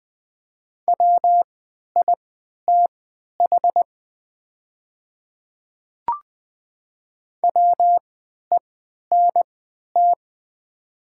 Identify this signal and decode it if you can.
Morse code, heard as a keyed tone, about 20 wpm, spaced as for 11 wpm, 690 Hz. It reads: WITH WENT
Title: 3x Character Spacing - Top 200 Words - 20wpm